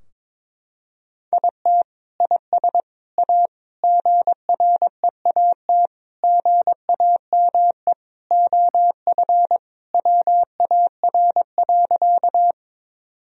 Morse code, heard as a keyed tone, about 22 wpm, 700 Hz. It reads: IT IS A GREAT GAME OF WAR.
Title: Sentences from Top 1000 Words - Review Part 3 - 22wpm